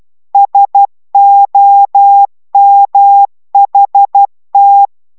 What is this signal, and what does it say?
Morse code, heard as a keyed tone, about 12 wpm, 800 Hz. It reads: SOMHT